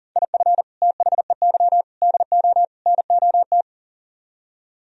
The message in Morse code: IF THEY DO NOT

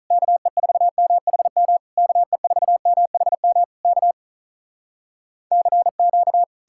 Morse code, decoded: KE4MHM KE4MHM K CQ

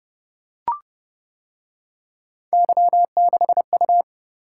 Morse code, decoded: Y6U

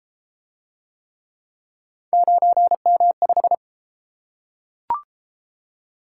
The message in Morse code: 9M5

